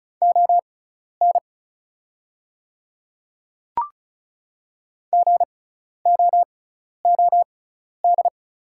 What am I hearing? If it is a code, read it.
Morse code, heard as a keyed tone, about 36 wpm, spaced as for 12 wpm, 690 Hz. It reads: ON GOOD
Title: 6x Character Spacing - Top 200 Words - 35wpm